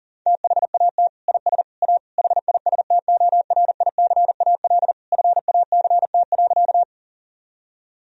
Morse code, heard as a keyed tone, about 40 wpm, 700 Hz. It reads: THAT IS A HISTORICAL FACT.
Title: Sentences from Top 1000 Words - Review Part 4 - 40wpm